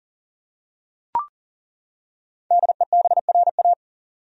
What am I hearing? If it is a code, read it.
Morse code, heard as a keyed tone, about 38 wpm, 700 Hz. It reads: DEBRA
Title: USA Common Names - 40wpm